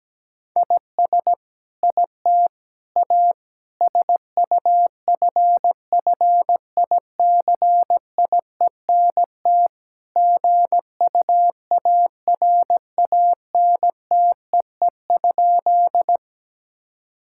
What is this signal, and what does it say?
Morse code, heard as a keyed tone, about 17 wpm, 700 Hz. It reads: IS IT A SUFFICIENT GUARANTEE?
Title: Sentences from Top 1600 Words 17wpm